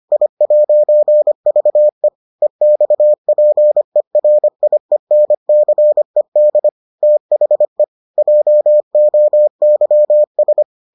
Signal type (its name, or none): Morse code